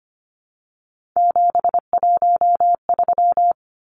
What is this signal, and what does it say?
Morse code, heard as a keyed tone, about 25 wpm, 700 Hz. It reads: 713
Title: Numbers - 3-Digit 25wpm